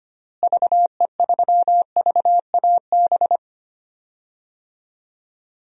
Morse code, heard as a keyed tone, about 25 wpm, 700 Hz. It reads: VE3VAB
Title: DXCC-100 Call Signs - 6 Characters - Repeated 3x - 25wpm